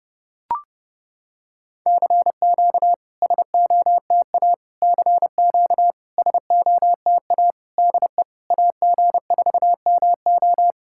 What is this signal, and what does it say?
Morse code, heard as a keyed tone, about 30 wpm, 690 Hz. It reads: CQ SOTA CQ SOTA DE AG4MO
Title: SOTA QSO Practice 30wpm